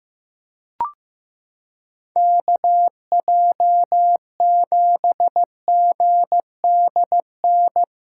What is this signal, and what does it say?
Morse code, heard as a keyed tone, about 15 wpm, 700 Hz. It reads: KJ7GDN
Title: USA Callsign DE Callsign - 6 Characters - 15wpm